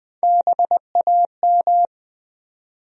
Morse code, heard as a keyed tone, about 20 wpm, 700 Hz. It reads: BAM
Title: USA Call Sign Postfixes - 3 Letters 20wpm